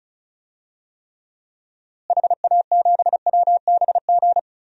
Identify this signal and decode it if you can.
Morse code, heard as a keyed tone, about 36 wpm, 700 Hz. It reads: HA7WBG
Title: DXCC-100 Call Signs - 6 Characters - Repeated 2x - 35wpm